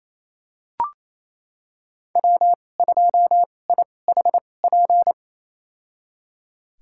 Morse code, heard as a keyed tone, about 28 wpm, spaced as for 20 wpm, 700 Hz. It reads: W2IHP
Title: USA Call Signs - 5 Characters - 2x Character Spacing - 28wpm